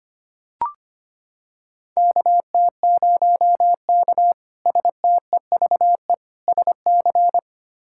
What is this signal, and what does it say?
Morse code, heard as a keyed tone, about 25 wpm, 700 Hz. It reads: KT0K STEVE SC
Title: K1USN SST QSO Practice - 25wpm